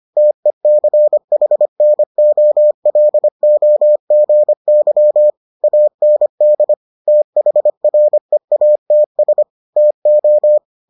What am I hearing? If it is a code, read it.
Morse code, heard as a keyed tone, about 25 wpm, 600 Hz. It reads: TECHNOLOGY AND THREATS TO